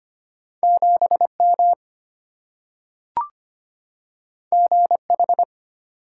Morse code, heard as a keyed tone, about 25 wpm, 690 Hz. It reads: 7M GH
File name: Two Letter-Number Combinations - 25wpm